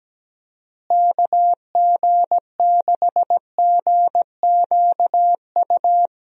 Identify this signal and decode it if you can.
Morse code, heard as a keyed tone, about 17 wpm, 700 Hz. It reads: KG6GQU
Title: USA Call Signs - 6 Characters - Repeated 3x - 17wpm